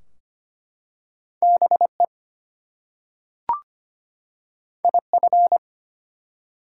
Morse code, heard as a keyed tone, about 25 wpm, 700 Hz. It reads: BE IF
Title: Two Letter Words - 25wpm